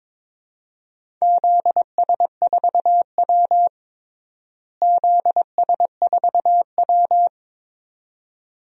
Morse code, heard as a keyed tone, about 22 wpm, 700 Hz. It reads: ZS4W ZS4W
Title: DXCC-100 Call Signs - 4 Characters - Repeated 2x - 22wpm